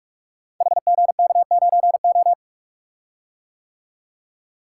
Morse code, heard as a keyed tone, about 44 wpm, 700 Hz. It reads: HGK9O
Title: Code Groups - 5 Letter:Numbers - 45wpm